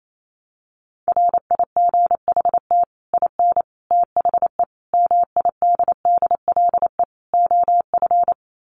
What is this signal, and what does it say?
Morse code, heard as a keyed tone, about 28 wpm, 700 Hz. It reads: RIGHT IN THE MIDDLE OF
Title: Sets of 5 Words 28wpm